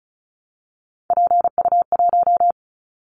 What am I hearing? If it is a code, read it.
Morse code, heard as a keyed tone, about 34 wpm, 700 Hz. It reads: PU1
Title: Callsign Prefixes for Top 50 DXCC - 35wpm